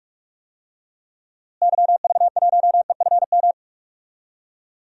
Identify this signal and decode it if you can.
Morse code, heard as a keyed tone, about 46 wpm, 700 Hz. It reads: YV1EFM
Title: DXCC-100 Call Signs - 6 Characters - Repeated 3x - 45wpm